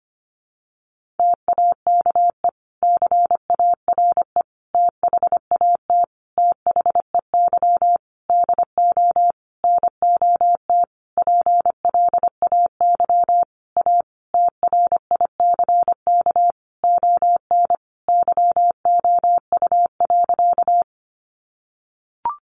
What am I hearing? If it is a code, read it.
Morse code, heard as a keyed tone, about 25 wpm, 700 Hz. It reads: TAKE CARE THAT THEY DO NOT PLAY A TRICK ON YOU.
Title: Sentences from Top 1500 Words 25wpm